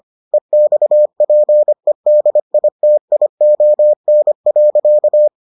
Morse code, heard as a keyed tone, about 25 wpm, 600 Hz. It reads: EXPEDITION.